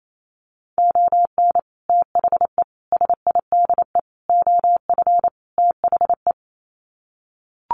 Morse code, heard as a keyed tone, about 28 wpm, 700 Hz. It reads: ON THE SIDE OF THE
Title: Sets of 5 Words 28wpm